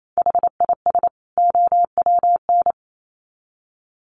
Morse code, heard as a keyed tone, about 28 wpm, 700 Hz. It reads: HIS OWN
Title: Sets of 2 Words 28wpm